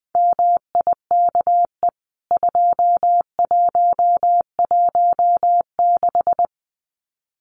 Morse code, heard as a keyed tone, about 20 wpm, 700 Hz. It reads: MIKE 2116
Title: CWT Exchange 20wpm